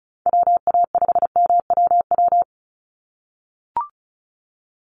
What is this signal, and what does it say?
Morse code, heard as a keyed tone, about 34 wpm, 700 Hz. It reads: WA5MWW
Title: USA Call Signs 35wpm